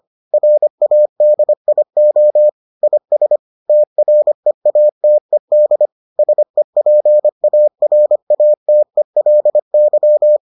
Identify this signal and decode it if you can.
Morse code, heard as a keyed tone, about 25 wpm, 600 Hz. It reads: RADIO IS TREATED SEPARATELY